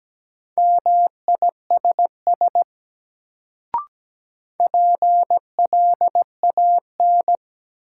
Morse code, heard as a keyed tone, about 17 wpm, 700 Hz. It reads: MISS PLAN
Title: Sentences from Top 500 Words - New Word Drill - 17wpm